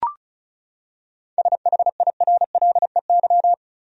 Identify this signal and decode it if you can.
Morse code, heard as a keyed tone, about 34 wpm, 700 Hz. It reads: SHIRLEY
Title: USA Common Names - 35wpm